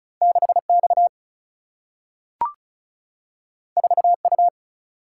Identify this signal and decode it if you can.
Morse code, heard as a keyed tone, about 36 wpm, 700 Hz. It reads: 6X 4U
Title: Single-Letter Single-Number 35wpm